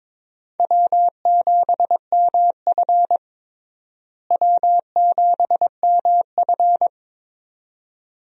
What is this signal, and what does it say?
Morse code, heard as a keyed tone, about 22 wpm, 700 Hz. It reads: W7MF W7MF
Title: USA Call Signs - 4 Characters - Repeated 2x - 22wpm